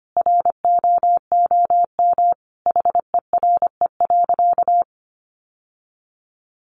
Morse code, heard as a keyed tone, about 25 wpm, 700 Hz. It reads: ROOM HERE.
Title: Sentences from Top 1300 Words 25wpm